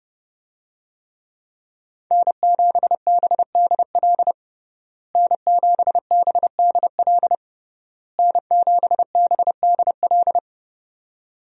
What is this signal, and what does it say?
Morse code, heard as a keyed tone, about 32 wpm, 700 Hz. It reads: N7BDL N7BDL N7BDL
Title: USA Call Signs - 5 Characters - Repeated 3x - 30wpm